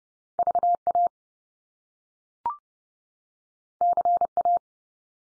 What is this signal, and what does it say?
Morse code, heard as a keyed tone, about 30 wpm, 700 Hz. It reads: VA CA
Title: US States 30WPM